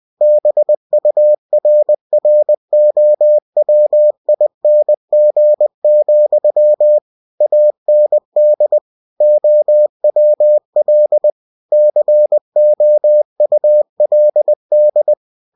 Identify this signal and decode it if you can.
Morse code, heard as a keyed tone, about 20 wpm, 600 Hz. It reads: BURROWING, AND OWL COULD
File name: cw-books / winnie-the-pooh / winnie-20wpm-page-0091